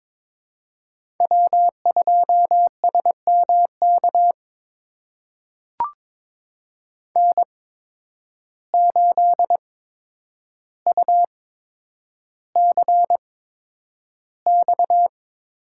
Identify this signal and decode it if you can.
Morse code, heard as a keyed tone, about 22 wpm, spaced as for 9 wpm, 700 Hz. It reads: W2SMK N8UCX
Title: USA Call Signs - 5 Characters - 8x Character Spacing - 22wpm